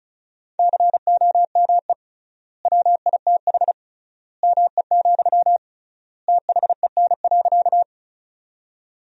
Morse code, heard as a keyed tone, about 36 wpm, 700 Hz. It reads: COME WITH ME, THEN.
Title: 3x Word-Spacing - Sentences from Top 200 Words - 35wpm